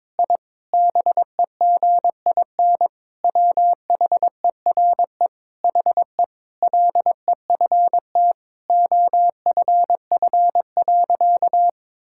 Morse code, heard as a keyed tone, about 22 wpm, 700 Hz. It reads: I BEGIN WHERE HE LEFT OFF.
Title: Sentences from Top 1000 Words - Review Part 6 - 22wpm